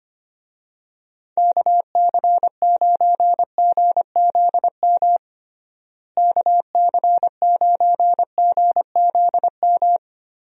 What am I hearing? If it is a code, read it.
Morse code, heard as a keyed tone, about 25 wpm, 700 Hz. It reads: KC9GZM KC9GZM
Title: USA Call Signs - 6 Characters - Repeated 3x - 25wpm